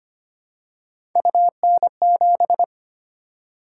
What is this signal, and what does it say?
Morse code, heard as a keyed tone, about 25 wpm, 700 Hz. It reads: UN7
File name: Three Letter-Number Combinations - 25wpm